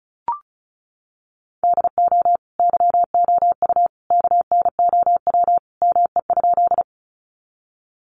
Morse code, heard as a keyed tone, about 34 wpm, 700 Hz. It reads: DO YOU KNOW ME?
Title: Sentences from Top 200 Words 35wpm